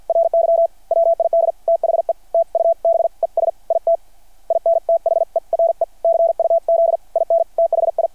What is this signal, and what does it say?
Morse code, heard as a keyed tone, about 51 wpm, spaced as for 45 wpm, 660 Hz. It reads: RY WID THE TUBES IT IN THERE CUZ IN THI
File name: archived / CRSnet